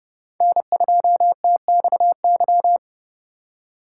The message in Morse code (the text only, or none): N2TXY